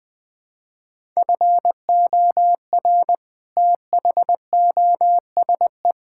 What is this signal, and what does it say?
Morse code, heard as a keyed tone, about 20 wpm, 700 Hz. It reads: FOR THOSE
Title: Sentences with 7-Word Phrases - 20wpm